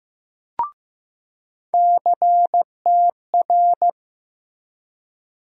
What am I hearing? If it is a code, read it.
Morse code, heard as a keyed tone, about 15 wpm, 700 Hz. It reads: CTR
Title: USA Call Sign Postfixes - 3 Letters 15wpm